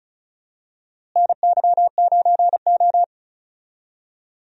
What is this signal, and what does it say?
Morse code, heard as a keyed tone, about 36 wpm, 700 Hz. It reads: NY9O
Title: USA Call Signs - 4 Characters - Repeated 3x - 35wpm